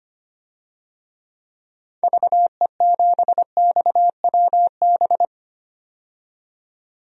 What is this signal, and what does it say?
Morse code, heard as a keyed tone, about 25 wpm, 700 Hz. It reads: VE7XWB
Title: DXCC-100 Call Signs - 6 Characters - 25wpm